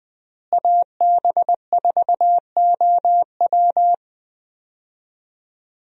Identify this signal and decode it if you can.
Morse code, heard as a keyed tone, about 20 wpm, 700 Hz. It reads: AB4OW
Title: USA Call Signs - 5 Characters - Repeated 3x - 20wpm